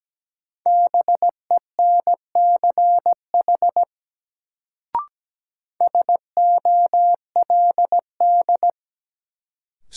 Morse code, heard as a keyed tone, about 17 wpm, 700 Hz. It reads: BENCH SOLD
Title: Sentences from Top 1400 Words - New Word Drill Alt - 17wpm